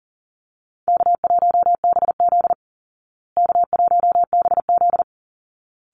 Morse code, heard as a keyed tone, about 40 wpm, 700 Hz. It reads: K1BZ K1BZ
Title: Call Signs Repeated 3x - 40wpm